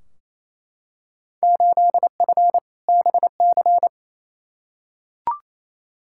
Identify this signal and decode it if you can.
Morse code, heard as a keyed tone, about 28 wpm, 700 Hz. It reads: 8F BC
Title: Field Day Exchanges Encore - 28wpm